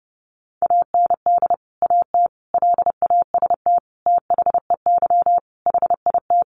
Morse code, heard as a keyed tone, about 30 wpm, 700 Hz. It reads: AND AT LAST THEY HIT
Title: Sentences from Top 1200 Words 30wpm